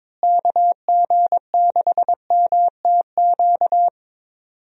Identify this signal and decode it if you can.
Morse code, heard as a keyed tone, about 22 wpm, 700 Hz. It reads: KG6MTQ
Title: USA Call Signs - 6 Characters - Repeated 3x - 22wpm